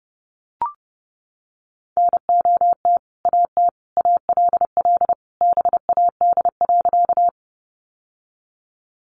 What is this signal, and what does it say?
Morse code, heard as a keyed tone, about 30 wpm, 700 Hz. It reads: NOT AT ALL BAD.
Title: Sentences from Top 700 Words 30wpm